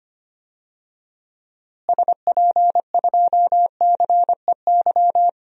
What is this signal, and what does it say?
Morse code, heard as a keyed tone, about 25 wpm, 700 Hz. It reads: SP2CEY
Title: DXCC-100 Call Signs - 6 Characters - Repeated 3x - 25wpm